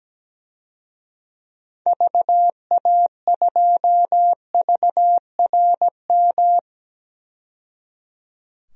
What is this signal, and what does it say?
Morse code, heard as a keyed tone, about 17 wpm, 700 Hz. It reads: VA2VRM
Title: DXCC-100 Call Signs - 6 Characters - 17wpm